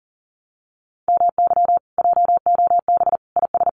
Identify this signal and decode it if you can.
Morse code, heard as a keyed tone, about 40 wpm, 700 Hz. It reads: MY JOB IS
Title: Common QSO Phrases Encore 40wpm